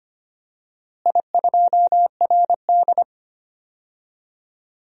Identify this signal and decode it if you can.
Morse code, heard as a keyed tone, about 25 wpm, 700 Hz. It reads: I2RD
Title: DXCC-100 Call Signs - 4 Characters - Repeated 3x - 25wpm